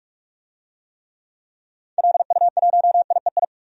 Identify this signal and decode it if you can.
Morse code, heard as a keyed tone, about 46 wpm, 700 Hz. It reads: LU1IEI
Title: DXCC-100 Call Signs - 6 Characters - Repeated 2x - 45wpm